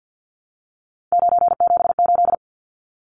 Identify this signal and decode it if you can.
Morse code, heard as a keyed tone, about 50 wpm, 700 Hz. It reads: 978
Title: Numbers - 3-Digit 50wpm